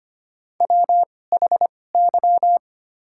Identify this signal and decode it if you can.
Morse code, heard as a keyed tone, about 25 wpm, spaced as for 18 wpm, 700 Hz. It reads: WHY
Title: 2x Character Spacing - Top 200 Words - 25wpm